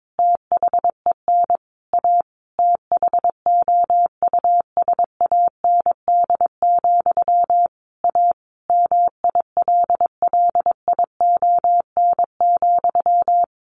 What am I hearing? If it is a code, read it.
Morse code, heard as a keyed tone, about 22 wpm, 700 Hz. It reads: THEN A THOUSAND, A MILLION,
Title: A Sound Of Thunder 22wpm